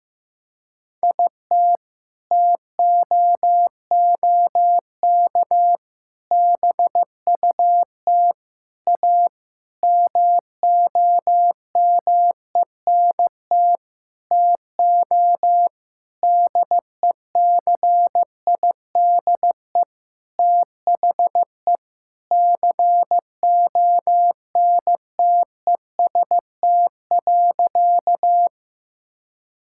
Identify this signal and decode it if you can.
Morse code, heard as a keyed tone, about 16 wpm, 700 Hz. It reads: IT TOOK BUT A MOMENT TO DECIDE THE CONTEST.
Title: Sentences from Top 1600 Words 15wpm